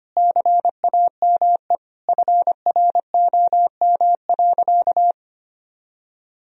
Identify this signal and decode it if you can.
Morse code, heard as a keyed tone, about 25 wpm, 700 Hz. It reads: CAME FROM.
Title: Sentences from Top 1000 Words - Review Part 4 - 25wpm